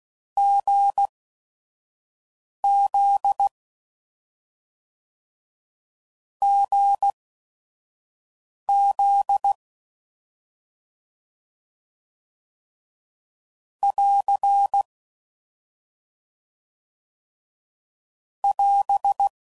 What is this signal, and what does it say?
Morse code, heard as a keyed tone, about 16 wpm, spaced as for 5 wpm, 800 Hz. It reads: GZ GZ <AR> <AS>